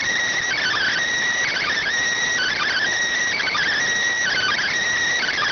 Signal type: Emergency Alert System (SAME) tones

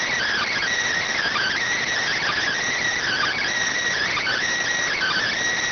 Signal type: Emergency Alert System (SAME) tones

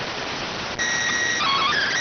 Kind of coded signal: SSTV (slow-scan television)

